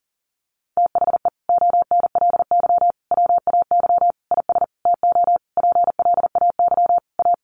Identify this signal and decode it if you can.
Morse code, heard as a keyed tone, about 40 wpm, 700 Hz. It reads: THE ONLY WAY IS TO PLAY A
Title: Sentences from Top 1500 Words 40wpm